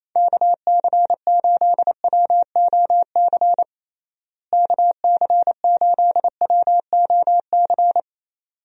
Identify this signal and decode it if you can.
Morse code, heard as a keyed tone, about 28 wpm, 700 Hz. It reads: KC8WOC KC8WOC
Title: USA Call Signs - 6 Characters - Repeated 2x - 28wpm